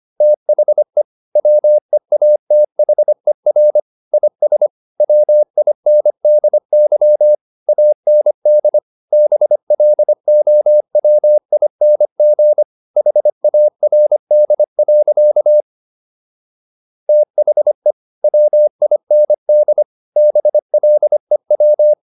Morse code, heard as a keyed tone, about 25 wpm, 600 Hz. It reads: THE WEATHER IS WINDY AND BLOWING HARD. THE WIND BLEW